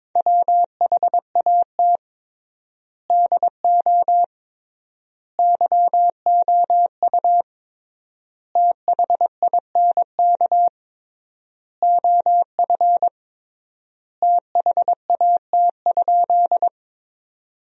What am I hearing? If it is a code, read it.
Morse code, heard as a keyed tone, about 22 wpm, 700 Hz. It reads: WHAT DO YOU THINK OF THAT?
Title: 3x Word-Spacing - Sentences from Top 200 Words - 22wpm